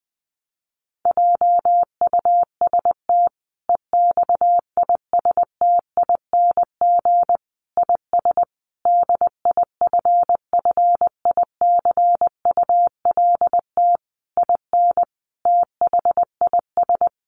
Morse code, heard as a keyed tone, about 20 wpm, 700 Hz. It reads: JUST EXISTING IS DIFFICULT IN THIS
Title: Sentences from Top 1500 Words 20wpm